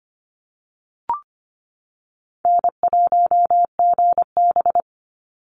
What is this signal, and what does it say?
Morse code, heard as a keyed tone, about 25 wpm, 700 Hz. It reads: N1GB
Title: NA SOTA Chaser Callsigns 25wpm